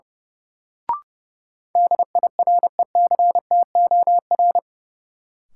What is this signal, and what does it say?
Morse code, heard as a keyed tone, about 32 wpm, 700 Hz. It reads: DIRECTOR
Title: Sentences from Top 1300 Words - New Word Drill - 30wpm